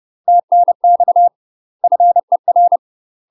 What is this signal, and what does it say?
Morse code, heard as a keyed tone, about 30 wpm, 700 Hz. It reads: TNX FER
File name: UK MASTER QSOs TRAINING/30WPM